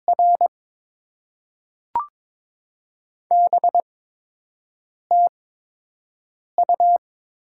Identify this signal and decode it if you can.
Morse code, heard as a keyed tone, about 22 wpm, spaced as for 6 wpm, 700 Hz. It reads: R BTU
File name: Top 20 QSO Elements - 8x Character Spacing - 22wpm